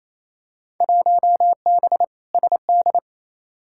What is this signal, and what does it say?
Morse code, heard as a keyed tone, about 28 wpm, 700 Hz. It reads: 1B SD
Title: Field Day Exchanges Encore - 28wpm